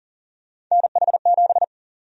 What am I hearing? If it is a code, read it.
Morse code, heard as a keyed tone, about 40 wpm, 700 Hz. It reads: NH7
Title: USA Call Sign Prefixes - 40wpm